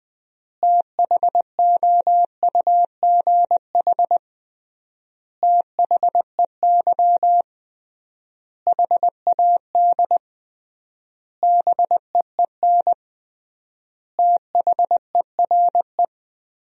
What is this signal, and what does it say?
Morse code, heard as a keyed tone, about 20 wpm, 700 Hz. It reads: THOUGH THEY HAD BEEN THERE
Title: 3x Word-Spacing - Sentences from Top 400 Words - 20wpm